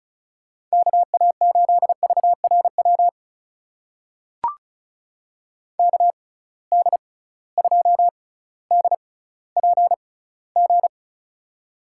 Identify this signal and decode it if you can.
Morse code, heard as a keyed tone, about 34 wpm, spaced as for 18 wpm, 700 Hz. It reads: KA8VRW KD2DPG
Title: USA Call Signs - 6 Characters - 6x Character Spacing - 35wpm